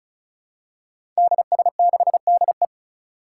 Morse code, heard as a keyed tone, about 35 wpm, 700 Hz. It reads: DS6DE
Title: Five Challenging Letter-Number Combinations 35wpm